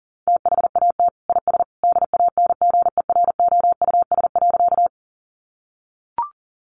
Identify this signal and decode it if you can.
Morse code, heard as a keyed tone, about 40 wpm, 700 Hz. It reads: THAT IS DANGEROUS.
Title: Sentences from Top 1200 Words 40wpm